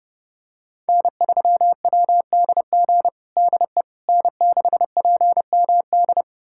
Morse code, heard as a keyed tone, about 32 wpm, 700 Hz. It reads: N3WDG DE N6PMD
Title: USA Callsign DE Callsign - 5 Characters - 30wpm